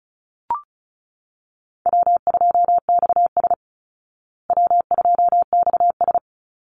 Morse code, heard as a keyed tone, about 34 wpm, 700 Hz. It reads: W2XS W2XS
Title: Call Signs Repeated 3x - 35wpm